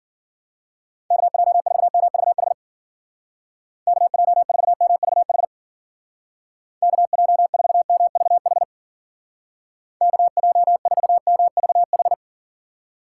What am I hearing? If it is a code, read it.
Morse code, heard as a keyed tone, about 51 wpm, 700 Hz. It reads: KJ4MVH KJ4MVH KJ4MVH KJ4MVH